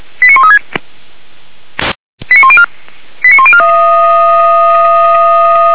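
Five-tone selective call (ZVEI), heard as quick runs of five tones, tones of about 70 ms. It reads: 89136 89104 89104
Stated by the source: ZVEI five-tone selective call 89200 received by SDR, logged by the Orlamünde fire brigade (feuerwehr-orlamuende).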